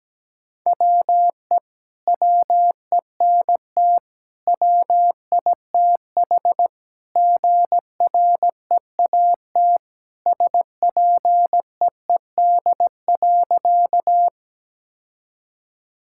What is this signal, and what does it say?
Morse code, heard as a keyed tone, about 17 wpm, 700 Hz. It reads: WE WENT WITH GREAT SPEED.
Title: Sentences from Top 1000 Words - Review Part 5 - 17wpm